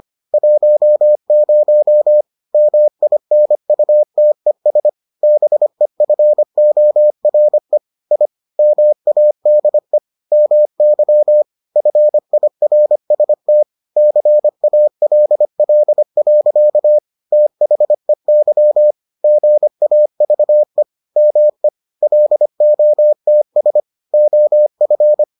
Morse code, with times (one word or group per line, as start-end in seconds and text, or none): 0.3-2.2 10
2.5-4.9 MINUTES
5.2-7.8 BEFORE
8.1-8.3 I
8.6-10.0 MADE
10.3-11.4 MY
11.8-13.6 FIRST
14.0-17.0 CALL.
17.3-18.9 THEY
19.2-20.8 GAVE
21.2-21.7 ME
22.0-23.8 LOTS
24.1-25.2 OF